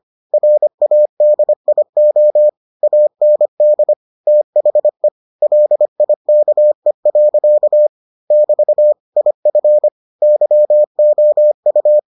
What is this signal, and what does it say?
Morse code, heard as a keyed tone, about 25 wpm, 600 Hz. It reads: RADIO AND THE LIKE. <BT> IF YOU